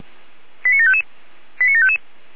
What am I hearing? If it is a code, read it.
Five-tone selective call (ZVEI), heard as quick runs of five tones, tones of about 70 ms. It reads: 89755 89755
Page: ZVEI